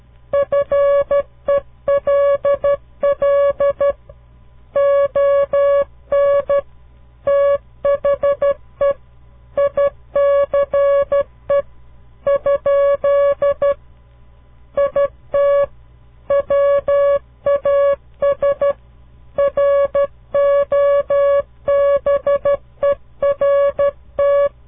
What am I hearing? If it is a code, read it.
Morse code, heard as a keyed tone, about 12 wpm, 560 Hz. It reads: FELL ON THE ICE ? IT WAS ROBERT
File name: CW Learning Programs / farnsworth / unit 2